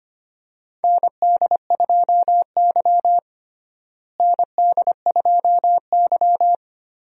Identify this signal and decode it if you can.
Morse code, heard as a keyed tone, about 25 wpm, 700 Hz. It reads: ND2Y ND2Y
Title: USA Call Signs - 4 Characters - Repeated 2x - 25wpm